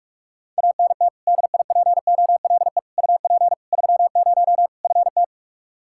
Morse code, heard as a keyed tone, about 41 wpm, 700 Hz. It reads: ANT DIPOLE UP 30 FT
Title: Practice QSO - Exchange 2 - Info - 45wpm